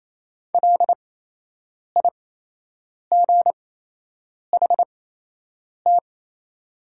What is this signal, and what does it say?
Morse code, heard as a keyed tone, about 28 wpm, spaced as for 8 wpm, 700 Hz. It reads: LIGHT
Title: 8x Character Spacing - Top 300 Words - 28wpm